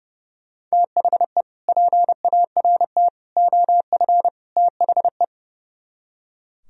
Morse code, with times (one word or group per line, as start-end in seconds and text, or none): 0.7-1.4 THE
1.7-3.1 PART
3.4-4.3 OF
4.6-5.2 THE